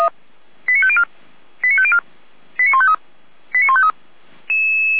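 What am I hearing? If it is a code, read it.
Five-tone selective call (ZVEI), heard as quick runs of five tones, tones of about 70 ms. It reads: 89504 89504 89153 89153
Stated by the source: ZVEI five-tone selective call received by SDR.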